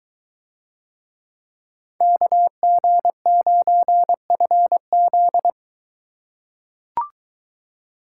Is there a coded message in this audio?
Yes: Morse code from 2.0 to 5.5 s, heard as a keyed tone, about 21 wpm, 700 Hz. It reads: KG9FZ